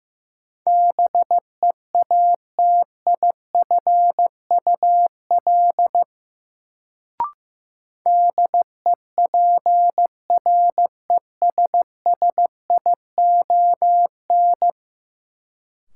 Morse code, heard as a keyed tone, about 15 wpm, 700 Hz. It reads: BEATIFUL DEPRESSION
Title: Sentences from Top 1300 Words - New Word Drill Alt - 15wpm